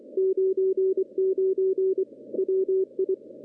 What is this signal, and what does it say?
Morse code, heard as a keyed tone, about 23 wpm, 380 Hz. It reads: 99 WI
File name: contests / CW-SS-k5tr- -0400